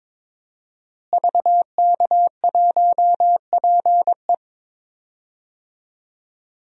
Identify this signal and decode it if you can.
Morse code, heard as a keyed tone, about 22 wpm, 700 Hz. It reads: VK1PE
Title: DXCC-100 Call Signs - 5 Characters - Repeated 3x - 22wpm